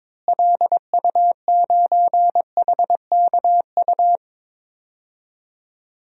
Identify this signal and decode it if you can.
Morse code, heard as a keyed tone, about 22 wpm, 700 Hz. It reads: LU9HKU
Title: DXCC-100 Call Signs - 6 Characters - Repeated 3x - 22wpm